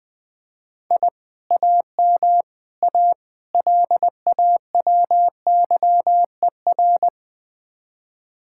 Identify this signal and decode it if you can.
Morse code, heard as a keyed tone, about 21 wpm, 700 Hz. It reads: I AM A LAWYER
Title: Speed-Racing Common QSO Phrases - 17wpm